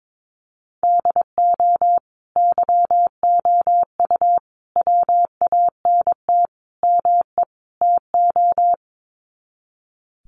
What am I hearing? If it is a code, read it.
Morse code, heard as a keyed tone, about 22 wpm, 700 Hz. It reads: DO YOU WANT ME TO
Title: Sets of 5 Words 22wpm